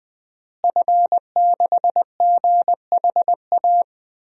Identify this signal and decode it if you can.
Morse code, heard as a keyed tone, about 20 wpm, 700 Hz. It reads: F6GHA
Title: DXCC-100 Call Signs - 5 Characters - Repeated 2x - 20wpm